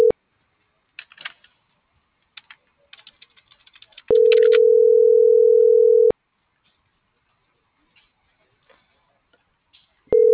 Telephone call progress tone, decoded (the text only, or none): ringback tone